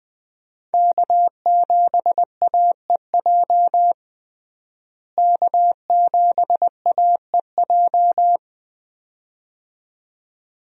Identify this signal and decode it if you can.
Morse code, heard as a keyed tone, about 20 wpm, 700 Hz. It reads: K7AEJ K7AEJ
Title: USA Call Signs - 5 Characters - Repeated 2x - 20wpm